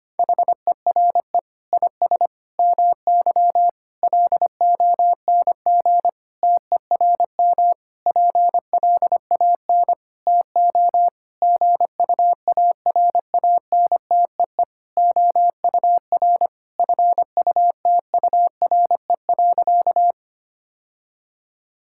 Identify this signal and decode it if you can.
Morse code, heard as a keyed tone, about 25 wpm, 700 Hz. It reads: HERE IS MY LONG TERM PLAN TO GUARANTEE OUR FUTURE.
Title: Sentences from Top 1600 Words 25wpm